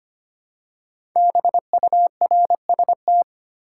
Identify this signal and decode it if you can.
Morse code, heard as a keyed tone, about 25 wpm, 700 Hz. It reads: BURST
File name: 5 Letter Words - First 3 Letters Given - 25wpm